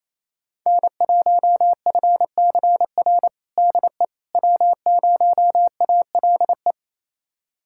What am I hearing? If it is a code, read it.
Morse code, heard as a keyed tone, about 28 wpm, 700 Hz. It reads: N1FCR DE W0ALE
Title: USA Callsign DE Callsign - 5 Characters - 28wpm